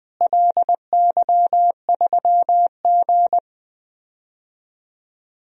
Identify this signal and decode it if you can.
Morse code, heard as a keyed tone, about 20 wpm, 700 Hz. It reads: LY3G
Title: DXCC-100 Call Signs - 4 Characters - Repeated 2x - 20wpm